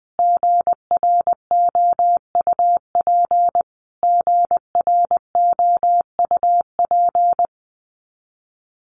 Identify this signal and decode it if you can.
Morse code, heard as a keyed tone, about 20 wpm, 700 Hz. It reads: GROUP GROUP
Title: Top 500 Words Repeated 20wpm